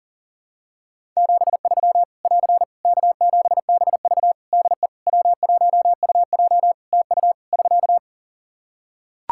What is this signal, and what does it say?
Morse code, heard as a keyed tone, about 40 wpm, 700 Hz. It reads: 73 <AR> K7BV DE W1UJ TU <SK>
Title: K1USN SST QSO Practice 40wpm